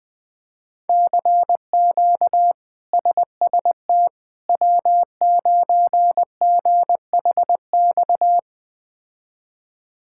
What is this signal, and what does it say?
Morse code, heard as a keyed tone, about 20 wpm, 700 Hz. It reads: CQ SST W9GHX
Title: K1USN SST QSO Practice - 20wpm